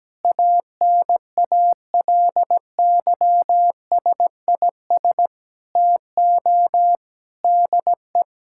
Morse code, heard as a keyed tone, about 17 wpm, 700 Hz. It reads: ANALYSIS TO DE
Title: Sentences from Top 1000 Words - Review Part 5 - 17wpm